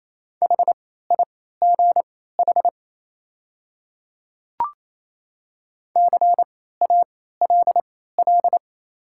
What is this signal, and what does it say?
Morse code, heard as a keyed tone, about 28 wpm, spaced as for 16 wpm, 700 Hz. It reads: HIGH CALL
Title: 3x Character Spacing - Top 200 Words - 28wpm